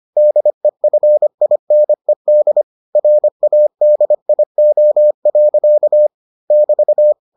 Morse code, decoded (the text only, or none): DEFINED RADIO. <BT>